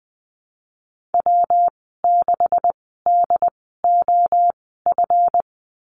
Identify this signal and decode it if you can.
Morse code, heard as a keyed tone, about 19 wpm, spaced as for 14 wpm, 700 Hz. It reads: W6DOF
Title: 154 Lesson 25g - Letter p - Call Signs 20wpm